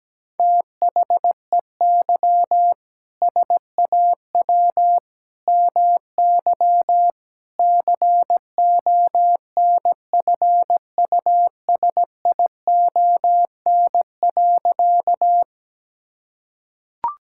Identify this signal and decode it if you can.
Morse code, heard as a keyed tone, about 17 wpm, 700 Hz. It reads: THEY SAW MY CONFUSION.
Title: Sentences from Top 1600 Words 17wpm